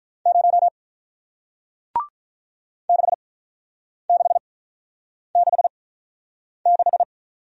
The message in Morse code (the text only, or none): 0 6666